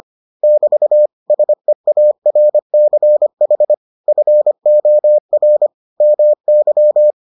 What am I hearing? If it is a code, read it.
Morse code, heard as a keyed tone, about 25 wpm, 600 Hz. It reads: <BT> SEARCH FOR MY